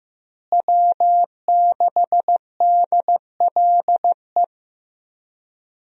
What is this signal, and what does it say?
Morse code, heard as a keyed tone, about 15 wpm, 700 Hz. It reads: W6DLE
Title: USA Call Signs - 5 Characters - Repeated 3x - 15wpm